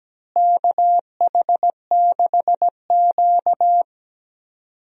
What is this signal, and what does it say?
Morse code, heard as a keyed tone, about 17 wpm, 700 Hz. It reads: KH6Q
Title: DXCC-100 Call Signs - 4 Characters - Repeated 3x - 17wpm